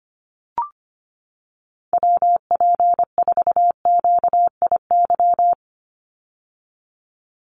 Morse code, heard as a keyed tone, about 25 wpm, 700 Hz. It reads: WP4QIY
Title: USA Call Signs 25wpm